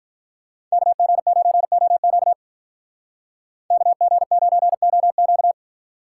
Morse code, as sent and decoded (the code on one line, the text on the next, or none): -.- --. ----. --- --.- / -.- --. ----. --- --.-
KG9OQ KG9OQ